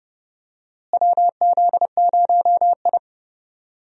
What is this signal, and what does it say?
Morse code, heard as a keyed tone, about 30 wpm, 700 Hz. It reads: WZ0I
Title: USA Call Signs - 4 Characters - Repeated 3x - 30wpm